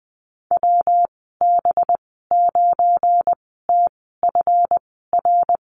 Morse code, heard as a keyed tone, about 20 wpm, spaced as for 15 wpm, 700 Hz. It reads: WB9TFR
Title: 193 Lesson 31g - Letter v - Call Signs 20wpm